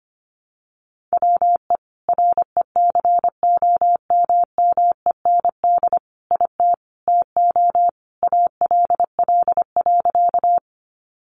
Morse code, heard as a keyed tone, about 25 wpm, 700 Hz. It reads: WE RECOMMEND IT TO ALL.
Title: Sentences from Top 1200 Words 25wpm